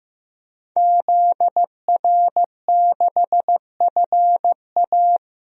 Morse code, heard as a keyed tone, about 15 wpm, 700 Hz. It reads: ZR6FA
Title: DXCC-100 Call Signs - 5 Characters - 15wpm